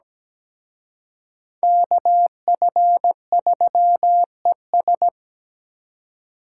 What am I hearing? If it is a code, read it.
Morse code, heard as a keyed tone, about 20 wpm, 700 Hz. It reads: KF3ES
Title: Speed Racing - USA 2x2 Call Signs 17wpm